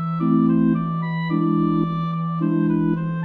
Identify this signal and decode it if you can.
There is no code.